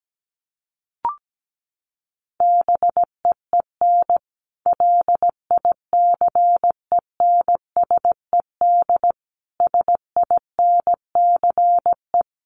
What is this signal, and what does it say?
Morse code, heard as a keyed tone, about 17 wpm, 700 Hz. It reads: BEEN LICENSED SINCE
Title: Common QSO Phrases 17wpm